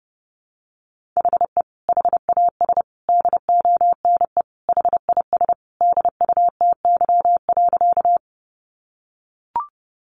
Morse code, heard as a keyed tone, about 30 wpm, 700 Hz. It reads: HE HAS DONE HIS DUTY.
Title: Sentences from Top 1500 Words 30wpm